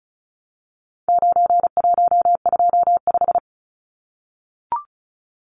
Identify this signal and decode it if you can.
Morse code, heard as a keyed tone, about 34 wpm, 700 Hz. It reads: 9125
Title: Numbers - 4-Digit 35wpm